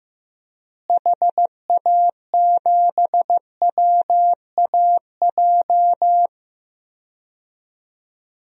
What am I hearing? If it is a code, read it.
Morse code, heard as a keyed tone, about 15 wpm, 700 Hz. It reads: HA7WAJ
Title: DXCC-100 Call Signs - 6 Characters - Repeated 2x - 15wpm